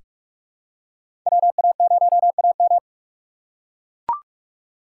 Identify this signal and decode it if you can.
Morse code, heard as a keyed tone, about 43 wpm, 700 Hz. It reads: WA0AM
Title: USA Call Signs - 5 Characters - 2x Character Spacing - 45wpm